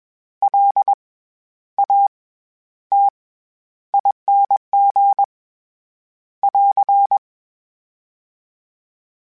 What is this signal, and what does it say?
Morse code, heard as a keyed tone, about 21 wpm, spaced as for 11 wpm, 800 Hz. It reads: LATING <AR>
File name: Special INGs-mp3s / Special INGs-mp3s / SlowToFaster ing @20